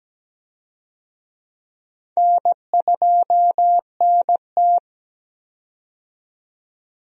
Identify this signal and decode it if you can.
Morse code, heard as a keyed tone, about 17 wpm, 700 Hz. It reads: N2NT
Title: USA Call Signs - 4 Characters - 17wpm